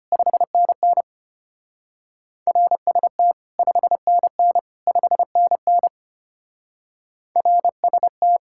Morse code, heard as a keyed tone, about 30 wpm, 700 Hz. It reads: 5NN RST 5NN 5NN RST
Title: Speed-Racing Common QSO Phrases - 25wpm